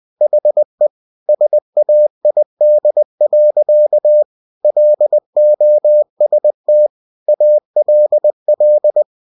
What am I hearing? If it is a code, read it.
Morse code, heard as a keyed tone, about 20 wpm, 600 Hz. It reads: HE SAID. LOST ALL
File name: cw-books / winnie-the-pooh / winnie-20wpm-page-0088